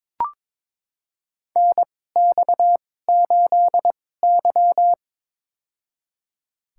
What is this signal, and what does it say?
Morse code, heard as a keyed tone, about 22 wpm, spaced as for 16 wpm, 700 Hz. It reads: NX8Y
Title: USA Call Signs - 4 Characters - 2x Character Spacing - 22wpm